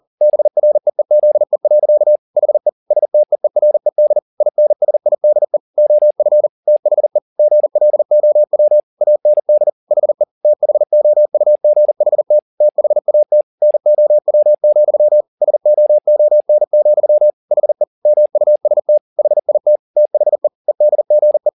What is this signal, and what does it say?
Morse code, heard as a keyed tone, about 40 wpm, 600 Hz. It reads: BREEZE. HE STEERED INSIDE OF THE GLOW AND HE THOUGHT THAT NOW, SOON, HE MUST HIT THE EDGE
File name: the old man and the sea 2 40wpm